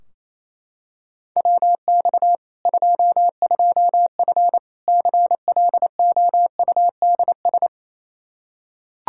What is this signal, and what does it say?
Morse code, heard as a keyed tone, about 28 wpm, 700 Hz. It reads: WX 22F CLOUDS
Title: Practice QSO - Exchange 2 - Info - 28wpm